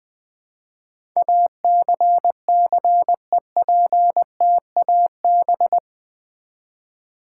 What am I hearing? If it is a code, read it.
Morse code, heard as a keyed tone, about 20 wpm, 700 Hz. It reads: ACCEPTAB
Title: Build A Word - 4 Letters At A Time - 10-Letter Words - 20wpm